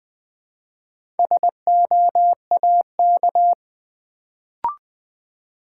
Morse code, heard as a keyed tone, about 20 wpm, 700 Hz. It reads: SOAK
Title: Build A Word - 1 Letter At A Time - 4-Letter Words - 20wpm